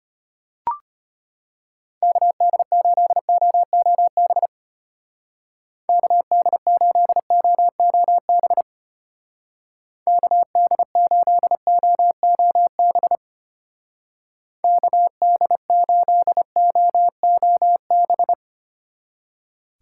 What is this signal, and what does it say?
Morse code, heard as a keyed tone, about 32 wpm, 700 Hz. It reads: KD8OOB KD8OOB KD8OOB KD8OOB